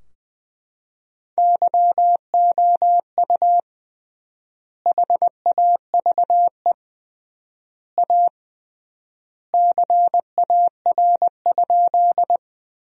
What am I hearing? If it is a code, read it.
Morse code, heard as a keyed tone, about 20 wpm, 700 Hz. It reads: YOU HAVE A CAR?
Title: 3x Word-Spacing - Sentences from Top 300 Words - 20wpm